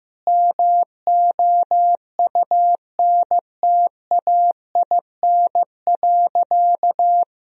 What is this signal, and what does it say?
Morse code, heard as a keyed tone, about 16 wpm, 700 Hz. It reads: MOUNTAIN.
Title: Sentences from Top 1000 Words - Review Part 1 - 15wpm